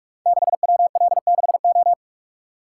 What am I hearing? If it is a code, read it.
Morse code, heard as a keyed tone, about 44 wpm, 700 Hz. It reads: 6WL6O
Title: Code Groups - 5 Letter:Numbers - 45wpm